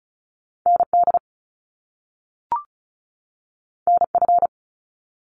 Morse code, heard as a keyed tone, about 34 wpm, 700 Hz. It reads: ND NF
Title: Top Two Letter Combinations in Words 35wpm